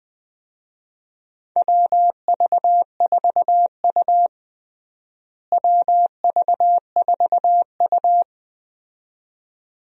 Morse code, heard as a keyed tone, about 20 wpm, 700 Hz. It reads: WV4U WV4U
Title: USA Call Signs - 4 Characters - Repeated 2x - 20wpm